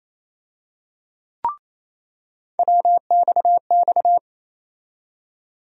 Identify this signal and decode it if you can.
Morse code, heard as a keyed tone, about 27 wpm, 700 Hz. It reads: WXX
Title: USA Call Sign Postfixes - 3 Letters 28wpm